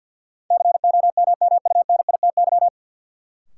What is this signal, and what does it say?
Morse code, heard as a keyed tone, about 51 wpm, 700 Hz. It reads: COMMUNITY
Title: Sentences from Top 1200 Words - New Word Drill Alt - 50wpm